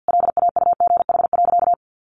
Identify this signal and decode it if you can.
Morse code, heard as a keyed tone, about 50 wpm, 700 Hz. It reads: LAUGH.
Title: Sentences from Top 600 Words 50wpm